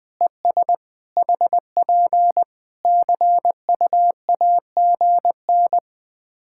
Morse code, heard as a keyed tone, about 19 wpm, 700 Hz. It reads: ES HP CUAGN
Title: Practice QSO - Exchange 4 - Ending - 20wpm